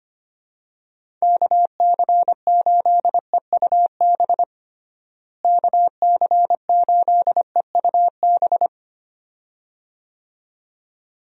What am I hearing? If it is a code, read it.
Morse code, heard as a keyed tone, about 25 wpm, 700 Hz. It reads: KC8EUB KC8EUB
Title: USA Call Signs - 6 Characters - Repeated 2x - 25wpm